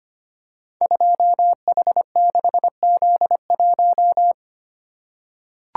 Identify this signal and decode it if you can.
Morse code, heard as a keyed tone, about 25 wpm, 700 Hz. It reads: 2H6Z1
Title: Code Groups - 5 Letter:Numbers - 25wpm